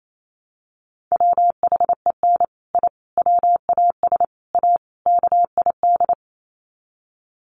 Morse code, heard as a keyed tone, about 28 wpm, 700 Hz. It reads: WHEN I WAS A KID
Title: Sets of 5 Words 28wpm